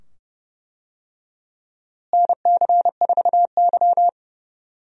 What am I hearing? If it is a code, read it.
Morse code, heard as a keyed tone, about 32 wpm, 700 Hz. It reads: NC4Y
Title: USA Call Signs - 4 Characters - Repeated 2x - 30wpm